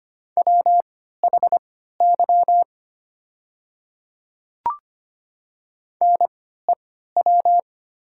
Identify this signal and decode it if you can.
Morse code, heard as a keyed tone, about 25 wpm, spaced as for 14 wpm, 700 Hz. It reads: WHY NEW
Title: 3x Character Spacing - Top 300 Words - 25wpm